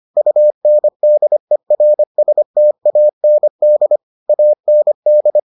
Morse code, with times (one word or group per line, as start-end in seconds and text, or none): 0.2-4.0 UNDERSTAND
4.3-5.4 AND